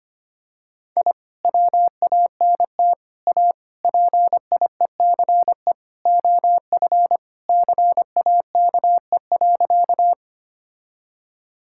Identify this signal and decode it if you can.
Morse code, heard as a keyed tone, about 25 wpm, 700 Hz. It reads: I WANT A PIECE OF CAKE.
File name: practice / 25wpm / Sentences with 3-Word Phrases - 25wpm